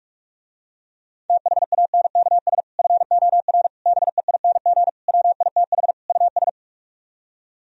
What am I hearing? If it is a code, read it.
Morse code, heard as a keyed tone, about 50 wpm, 700 Hz. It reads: THANKS FOR BEING WITH US